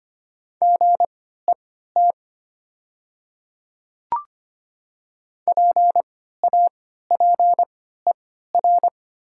Morse code, heard as a keyed tone, about 25 wpm, spaced as for 14 wpm, 700 Hz. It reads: GET PAPER
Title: 3x Character Spacing - Top 300 Words - 25wpm